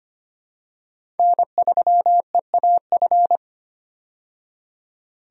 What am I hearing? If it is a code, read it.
Morse code, heard as a keyed tone, about 25 wpm, 700 Hz. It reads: N3EAF
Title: USA Call Signs - 5 Characters - 25wpm